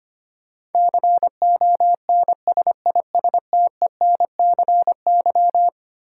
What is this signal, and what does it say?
Morse code, heard as a keyed tone, about 25 wpm, 700 Hz. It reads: CONSISTENCY
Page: morse-code-ninja-practice-25wpm